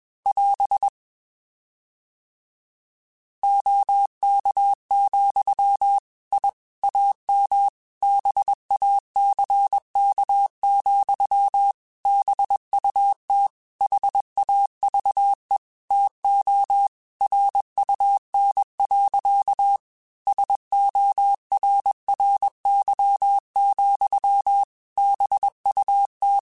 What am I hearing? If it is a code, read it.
Morse code, heard as a keyed tone, about 21 wpm, 800 Hz. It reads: <AS> OK, I AM BACK, BUT HAVE TO RUN. SORRY, BUT
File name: QSOs / QSOs-mp3s / WA7QOX 2@20